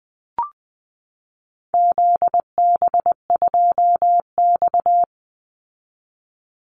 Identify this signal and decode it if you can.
Morse code, heard as a keyed tone, about 20 wpm, 700 Hz. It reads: ZB2X
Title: Call Signs Repeated 2x - 20wpm